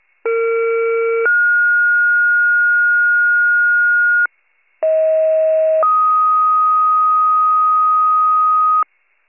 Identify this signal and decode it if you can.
Two-tone page, heard as a one-second tone, then a three-second tone, about 460 then 1430 Hz, about 640 then 1250 Hz, with noise. It is heard from 0.2 to 8.8 s.
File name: pager / 38 FIRE / wcalert